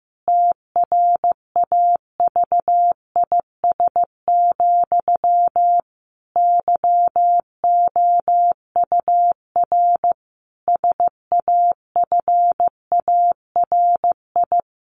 Morse code, heard as a keyed tone, about 15 wpm, 700 Hz. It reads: TRAVIS, YOUR SAFARI
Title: A Sound Of Thunder 15wpm